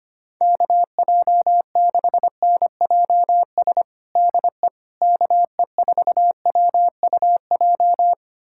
Morse code, heard as a keyed tone, about 25 wpm, 700 Hz. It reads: KJ6NJS DE KE4WUJ
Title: USA Callsign DE Callsign 25wpm